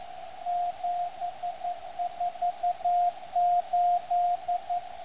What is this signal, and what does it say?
Morse code, heard as a keyed tone, about 11 wpm, 710 Hz. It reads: 748
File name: lu4aao / LU7AA 14ago2012 1012LU 7022KHz FL angos